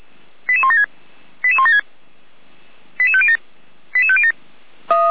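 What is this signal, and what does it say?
Five-tone selective call (ZVEI), heard as quick runs of five tones, tones of about 70 ms. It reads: 88176 88176 88507 88507
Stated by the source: ZVEI five-tone selective call received on SDR.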